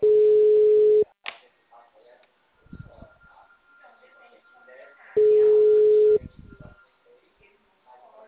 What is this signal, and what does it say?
Ringback tone (Europe, 425 Hz), with speech.
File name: out- -2006- - - .44745